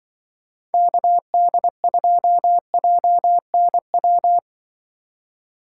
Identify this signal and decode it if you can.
Morse code, heard as a keyed tone, about 25 wpm, 700 Hz. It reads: KD2JNW
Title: Speed Racing - USA Call Signs 20wpm